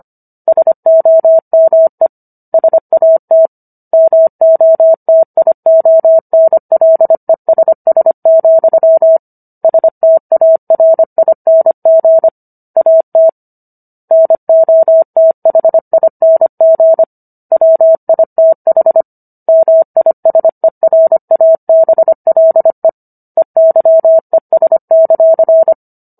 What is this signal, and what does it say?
Morse code, heard as a keyed tone, about 25 wpm, 650 Hz. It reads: SOME SAT MOTIONLESS, STARING AT NOTHING WITH MISERABLE EYES;